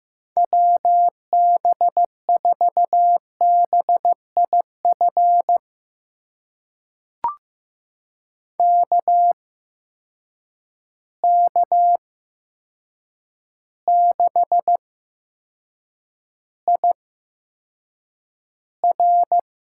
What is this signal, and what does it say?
Morse code, heard as a keyed tone, about 15 wpm, spaced as for 7 wpm, 700 Hz. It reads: WB4BIF KK6IR
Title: USA Call Signs - 6 Characters - 8x Character Spacing - 15wpm